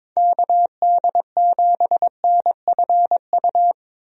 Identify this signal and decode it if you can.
Morse code, heard as a keyed tone, about 22 wpm, 700 Hz. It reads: KD7NFU
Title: USA Call Signs - 6 Characters - Repeated 2x - 22wpm